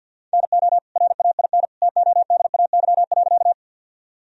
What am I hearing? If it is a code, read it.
Morse code, heard as a keyed tone, about 51 wpm, 700 Hz. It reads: NO RAIN TODAY.
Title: Sentences from Top 1000 Words - Review Part 1 - 50wpm